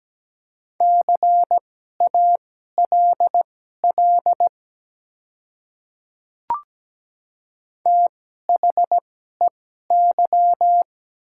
Morse code, heard as a keyed tone, about 17 wpm, spaced as for 12 wpm, 700 Hz. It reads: CALL THEY
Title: 2x Character Spacing - Top 100 Words - 17wpm